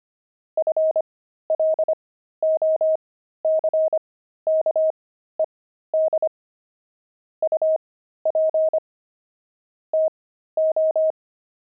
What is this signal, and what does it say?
Morse code, heard as a keyed tone, about 25 wpm, spaced as for 13 wpm, 630 Hz. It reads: FLOCKED UP TO